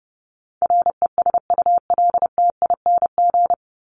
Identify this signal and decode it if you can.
Morse code, heard as a keyed tone, about 30 wpm, 700 Hz. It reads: RESULTING
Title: ING Suffix - Top 100 Words with Base Words 30wpm